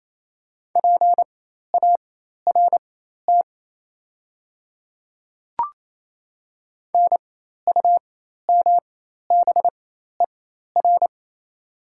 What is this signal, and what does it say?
Morse code, heard as a keyed tone, about 27 wpm, spaced as for 13 wpm, 690 Hz. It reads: PART NUMBER
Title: 4x Character Spacing - Top 200 Words - 28wpm